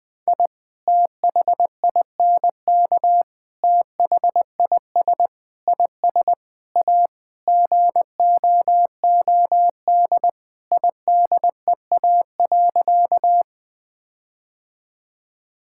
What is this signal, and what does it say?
Morse code, heard as a keyed tone, about 20 wpm, 700 Hz. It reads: I THINK THIS IS A GOOD IDEA.
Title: Sentences with 5-Word Phrases - 20wpm